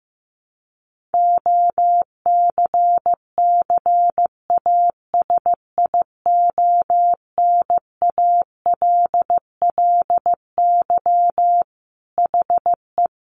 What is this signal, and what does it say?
Morse code, heard as a keyed tone, about 15 wpm, 700 Hz. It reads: OCCASIONALLY HE
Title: Sentences from Top 1300 Words 15wpm